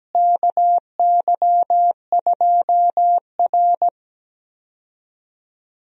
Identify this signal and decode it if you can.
Morse code, heard as a keyed tone, about 17 wpm, 700 Hz. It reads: KY2R
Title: USA Call Signs - 4 Characters - Repeated 3x - 17wpm